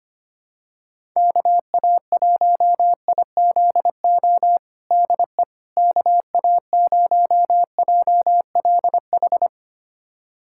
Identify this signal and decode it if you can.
Morse code, heard as a keyed tone, about 25 wpm, 700 Hz. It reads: KA1IZO DE KA0JLH
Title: USA Callsign DE Callsign - 6 Characters - 25wpm